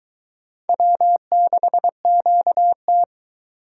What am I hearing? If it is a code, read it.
Morse code, heard as a keyed tone, about 22 wpm, 700 Hz. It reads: W6QT